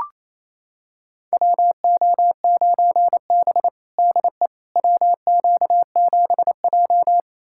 Morse code, heard as a keyed tone, about 28 wpm, 700 Hz. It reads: WO9B DE WQ7J
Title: USA Callsign DE Callsign - 4 Characters - 28wpm